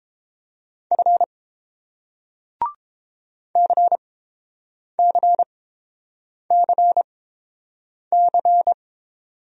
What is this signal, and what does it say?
Morse code, heard as a keyed tone, about 30 wpm, spaced as for 7 wpm, 700 Hz. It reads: F CCCC